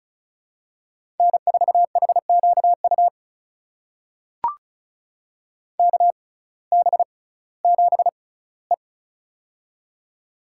Morse code, heard as a keyed tone, about 36 wpm, spaced as for 19 wpm, 700 Hz. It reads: N4HQU KB7E
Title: USA Call Signs - 6x Character Spacing - 35wpm